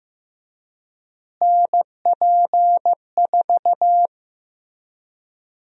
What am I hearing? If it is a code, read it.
Morse code, heard as a keyed tone, about 15 wpm, 700 Hz. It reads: NP4
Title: USA Call Sign Prefixes - Repeated 2x - 15wpm